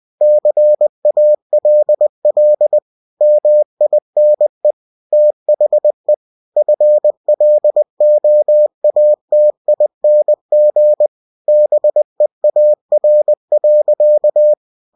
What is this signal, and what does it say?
Morse code, heard as a keyed tone, about 20 wpm, 600 Hz. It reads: CALL MINE THE FLOATING BEAR.
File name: cw-books / winnie-the-pooh / winnie-20wpm-page-0095